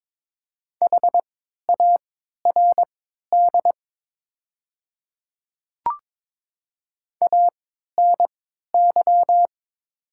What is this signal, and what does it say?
Morse code, heard as a keyed tone, about 22 wpm, spaced as for 12 wpm, 700 Hz. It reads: HARD ANY
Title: 3x Character Spacing - Top 300 Words - 22wpm